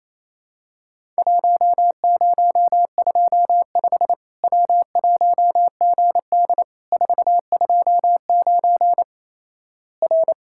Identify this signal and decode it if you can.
Morse code, heard as a keyed tone, about 28 wpm, 700 Hz. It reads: 1025 W1GD 429 R
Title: Worked All Europe DX Contest - QTC Exchange - 28wpm